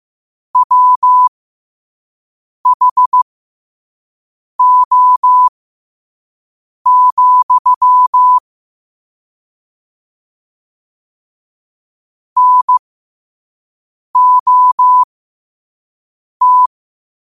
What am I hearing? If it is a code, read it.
Morse code, heard as a keyed tone, about 15 wpm, spaced as for 5 wpm, 1000 Hz. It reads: WHO, NOT
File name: qotdfiles / 5wpm